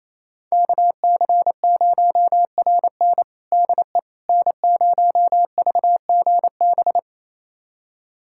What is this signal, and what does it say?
Morse code, heard as a keyed tone, about 28 wpm, 700 Hz. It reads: KC0RN DE N0VGB
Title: USA Callsign DE Callsign 28wpm